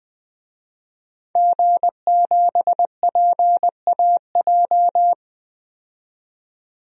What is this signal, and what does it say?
Morse code, heard as a keyed tone, about 20 wpm, 700 Hz. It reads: G7PAJ
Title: DXCC-100 Call Signs - 5 Characters - 20wpm